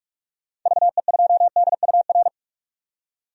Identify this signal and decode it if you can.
Morse code, heard as a keyed tone, about 46 wpm, 700 Hz. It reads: VE2DUR
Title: DXCC-100 Call Signs - 6 Characters - 45wpm